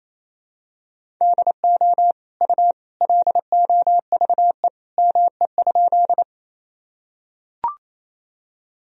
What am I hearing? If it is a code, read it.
Morse code, heard as a keyed tone, about 28 wpm, 700 Hz. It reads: DO U LOVE ME?
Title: Sentences from Top 400 Words and Abbreviations 28wpm